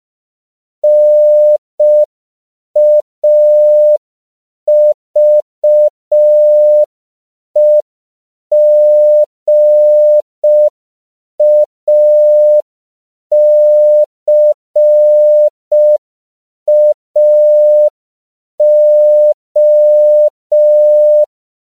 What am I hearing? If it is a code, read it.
Morse code, heard as a keyed tone, about 5 wpm, 600 Hz. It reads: NAVEGACAO
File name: CW / 5PPM